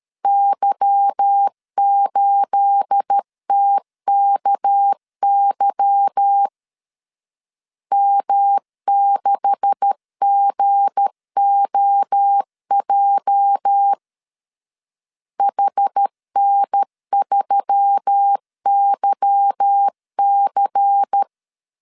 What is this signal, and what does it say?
Morse code, heard as a keyed tone, about 13 wpm, 800 Hz. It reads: Y8TKY M6GOJ HN3YC
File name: CW / DET / 1 9 1500C S A 12wpm 3800